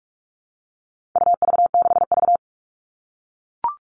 Morse code, heard as a keyed tone, about 44 wpm, 700 Hz. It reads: UV6V
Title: All Cut Numbers - 4-Digit 45wpm